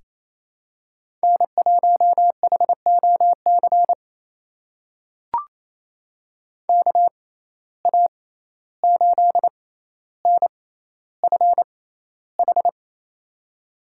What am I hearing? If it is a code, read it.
Morse code, heard as a keyed tone, about 28 wpm, spaced as for 14 wpm, 700 Hz. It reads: N1HOC KA8NFH
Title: USA Call Signs - 6x Character Spacing - 28wpm